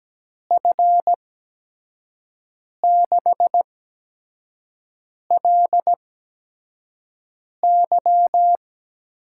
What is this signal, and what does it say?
Morse code, heard as a keyed tone, about 17 wpm, spaced as for 5 wpm, 700 Hz. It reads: F6LY
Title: USA Call Signs - 8x Character Spacing - 17wpm